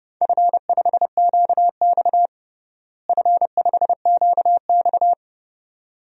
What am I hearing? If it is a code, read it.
Morse code, heard as a keyed tone, about 30 wpm, 700 Hz. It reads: F5QX F5QX
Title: DXCC-100 Call Signs - 4 Characters - Repeated 2x - 30wpm